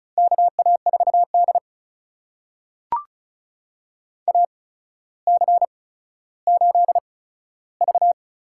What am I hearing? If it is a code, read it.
Morse code, heard as a keyed tone, about 34 wpm, spaced as for 15 wpm, 700 Hz. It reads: KA4D AC8V
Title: USA Call Signs - 4 Characters - 8x Character Spacing - 35wpm